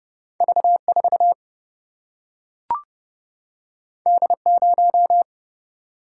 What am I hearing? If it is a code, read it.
Morse code, heard as a keyed tone, about 30 wpm, 700 Hz. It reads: V 4 D 0
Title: Single-Letter Single-Number 30wpm